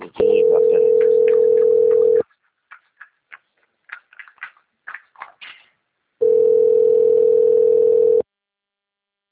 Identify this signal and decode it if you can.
Ringback tone (North America), 440 + 480 Hz.